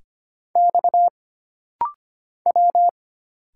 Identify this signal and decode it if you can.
Morse code, heard as a keyed tone, about 25 wpm, 700 Hz. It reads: X W
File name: Single Character (Letter:Number:Punctuation:Prosign) - Rapid-Fire - 25wpm